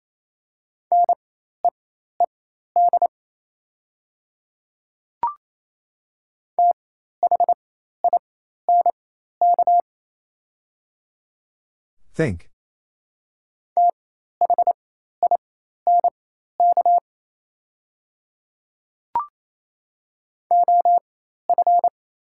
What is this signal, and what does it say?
Morse code, heard as a keyed tone, about 28 wpm, spaced as for 13 wpm, 700 Hz. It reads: NEED THINK THINK OF
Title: 4x Character Spacing - Top 200 Words - 28wpm